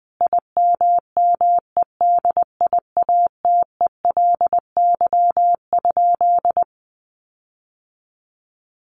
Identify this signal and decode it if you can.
Morse code, heard as a keyed tone, about 20 wpm, 700 Hz. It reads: IMMEDIATELY?
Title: Sentences from Top 1300 Words 20wpm